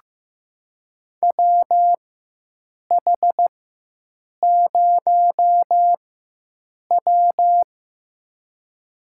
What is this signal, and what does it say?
Morse code, heard as a keyed tone, about 15 wpm, spaced as for 7 wpm, 700 Hz. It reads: WH0W
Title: USA Call Signs - 4x Character Spacing - 15wpm